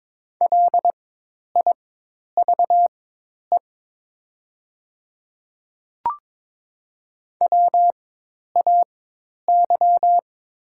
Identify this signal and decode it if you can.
Morse code, heard as a keyed tone, about 22 wpm, spaced as for 10 wpm, 690 Hz. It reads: LIVE WAY
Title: 4x Character Spacing - Top 300 Words - 22wpm